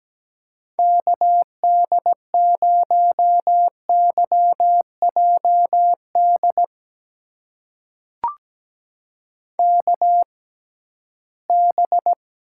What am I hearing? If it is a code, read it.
Morse code, heard as a keyed tone, about 17 wpm, spaced as for 13 wpm, 700 Hz. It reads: KD0YJD KB
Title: USA Call Signs - 6 Characters - 6x Character Spacing - 17wpm